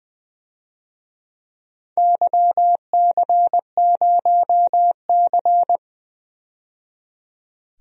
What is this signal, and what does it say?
Morse code, heard as a keyed tone, about 20 wpm, 700 Hz. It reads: YC0C
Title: DXCC-100 Call Signs - 4 Characters - 20wpm